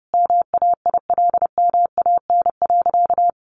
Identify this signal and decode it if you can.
Morse code, heard as a keyed tone, about 30 wpm, 700 Hz. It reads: MAILMAN.
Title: Easy Phrases 30wpm